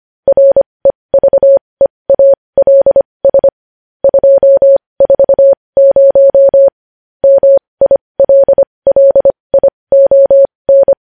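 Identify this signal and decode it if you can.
Morse code, heard as a keyed tone, about 25 wpm, spaced as for 20 wpm, 550 Hz. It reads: REVEALS 240 MILLION
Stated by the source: Morse code news headlines practice at 20.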